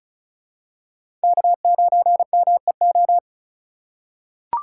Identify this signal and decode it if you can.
Morse code, heard as a keyed tone, about 36 wpm, 700 Hz. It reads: K9MEO
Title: USA Call Signs - 5 Characters - 2x Character Spacing - 35wpm